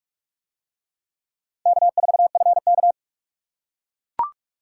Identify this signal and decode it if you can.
Morse code, heard as a keyed tone, about 40 wpm, 700 Hz. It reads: K4FK